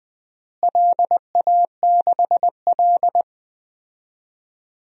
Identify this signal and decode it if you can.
Morse code, heard as a keyed tone, about 20 wpm, 700 Hz. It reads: LA6L
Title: DXCC-100 Call Signs - 4 Characters - Repeated 3x - 20wpm